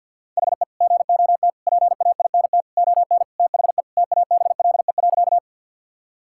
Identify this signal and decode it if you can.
Morse code, heard as a keyed tone, about 50 wpm, 700 Hz. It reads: HE GOT PAINT ON THE TABLE.
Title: Sentences from Top 1000 Words - Review Part 6 - 50wpm